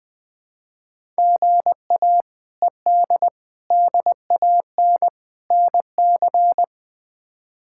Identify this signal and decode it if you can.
Morse code, heard as a keyed tone, about 21 wpm, 700 Hz. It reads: GA ED DAN NC
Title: K1USN SST QSO Practice - 20wpm